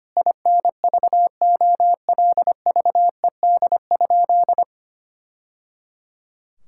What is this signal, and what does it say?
Morse code, heard as a keyed tone, about 25 wpm, 700 Hz. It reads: INVOLVED?
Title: Sentences from Top 1000 Words - Review Part 5 - 25wpm